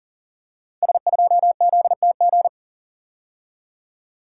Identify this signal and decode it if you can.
Morse code, heard as a keyed tone, about 40 wpm, 700 Hz. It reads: S2ZTG
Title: Code Groups - 5 Letter:Numbers - 40wpm